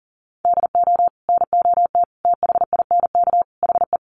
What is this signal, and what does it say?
Morse code, heard as a keyed tone, about 40 wpm, 700 Hz. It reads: DO NOT THINK HE
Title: Sets of 4 Words 40wpm